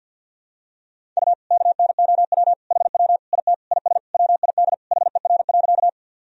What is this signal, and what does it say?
Morse code, heard as a keyed tone, about 51 wpm, 700 Hz. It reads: U KNOW HW IT IS WID HER.
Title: Sentences from Top 100 Words and Abbreviations 50wpm